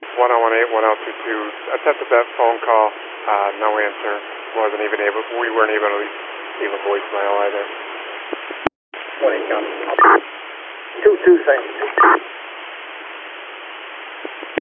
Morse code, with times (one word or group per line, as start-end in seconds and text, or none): none